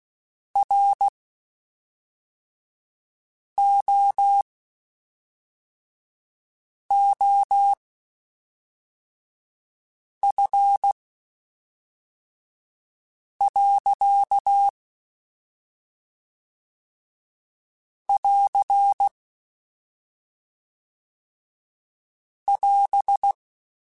Morse code, heard as a keyed tone, about 16 wpm, spaced as for 3 wpm, 800 Hz. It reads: ROOF. <AR> <AS>
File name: Spaced Characters / Spaced Characters-mp3s / Weather4 24 sp @5